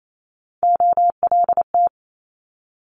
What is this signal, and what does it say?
Morse code, heard as a keyed tone, about 28 wpm, 700 Hz. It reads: OLT
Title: Multi-letter Phonemes 28wpm